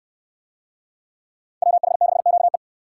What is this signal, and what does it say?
Morse code, heard as a keyed tone, about 66 wpm, 700 Hz. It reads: WH6QE